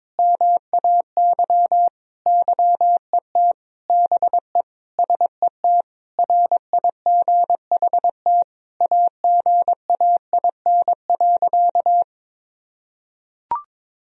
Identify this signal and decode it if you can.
Morse code, heard as a keyed tone, about 22 wpm, 700 Hz. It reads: MAY YET BE SET RIGHT AGAIN.
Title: Sentences from Top 1000 Words - Review Part 1 - 22wpm